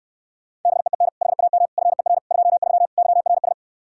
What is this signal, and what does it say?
Morse code, heard as a keyed tone, about 70 wpm, 700 Hz. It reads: BEEN HAM FER 23 YRS